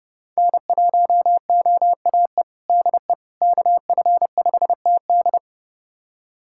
Morse code, heard as a keyed tone, about 30 wpm, 700 Hz. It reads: N1OAE DE KF5TD
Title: USA Callsign DE Callsign - 5 Characters - 30wpm